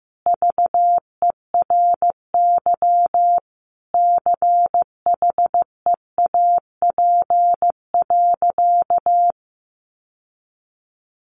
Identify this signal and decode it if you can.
Morse code, heard as a keyed tone, about 15 wpm, 700 Hz. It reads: VERY CHEAP.
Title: Sentences from Top 1200 Words 15wpm